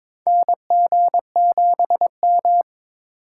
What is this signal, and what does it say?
Morse code, heard as a keyed tone, about 22 wpm, 700 Hz. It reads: NG7M
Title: USA Call Signs - 4x Character Spacing - 22wpm